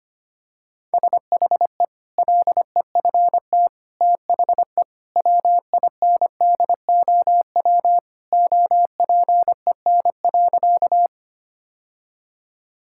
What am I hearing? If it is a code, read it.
Morse code, heard as a keyed tone, about 25 wpm, 700 Hz. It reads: SHE LEFT THE WINDOW OPEN.
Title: Sentences from Top 1600 Words 25wpm